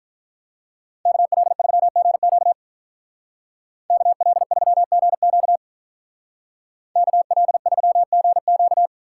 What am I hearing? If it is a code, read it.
Morse code, heard as a keyed tone, about 44 wpm, 700 Hz. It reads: KL3GQ KL3GQ KL3GQ